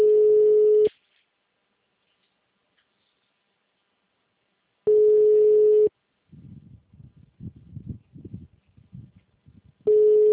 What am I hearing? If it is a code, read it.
Ringback tone (Europe, 425 Hz).